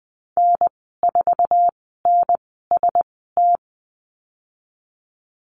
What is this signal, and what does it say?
Morse code, heard as a keyed tone, about 20 wpm, spaced as for 14 wpm, 700 Hz. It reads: N4NST
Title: 082 Lesson 14g - Number 2 - Call Signs 20wpm